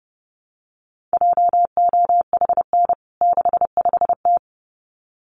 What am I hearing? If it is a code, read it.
Morse code, heard as a keyed tone, about 30 wpm, 700 Hz. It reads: JOHN 65T
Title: CWT Exchange 30wpm